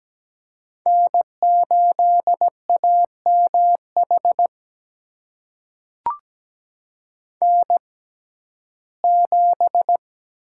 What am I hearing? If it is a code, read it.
Morse code, heard as a keyed tone, about 17 wpm, spaced as for 12 wpm, 700 Hz. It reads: N8AMH N7
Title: USA Call Signs - 5 Characters - 6x Character Spacing - 17wpm